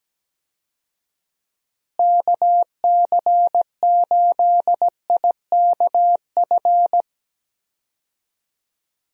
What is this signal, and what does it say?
Morse code, heard as a keyed tone, about 17 wpm, 700 Hz. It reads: KC8IKF
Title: USA Call Signs - 6 Characters - Repeated 3x - 17wpm